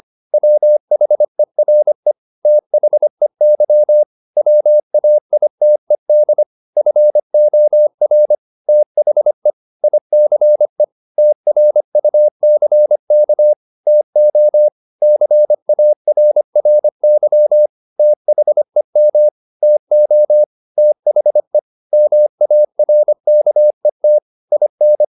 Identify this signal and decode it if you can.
Morse code, heard as a keyed tone, about 25 wpm, 600 Hz. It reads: WHERE THEY WAITED FOR THE ICE TRUCK TO CARRY THEM TO THE MARKET IN
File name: the old man and the sea part 1 25wpm